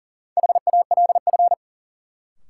Morse code, heard as a keyed tone, about 40 wpm, 700 Hz. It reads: HALF
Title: Sets of 3 Words - New Word Drill - 40wpm